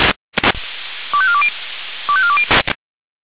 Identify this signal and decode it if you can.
Five-tone selective call (ZVEI), heard as quick runs of five tones, tones of about 70 ms. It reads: 25420 25420